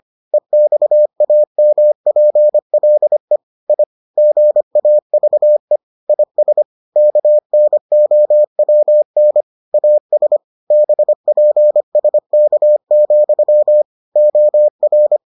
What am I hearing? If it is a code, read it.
Morse code, heard as a keyed tone, about 25 wpm, 600 Hz. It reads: EXAMPLE I GAVE IS KNOWN AS BPSK, OR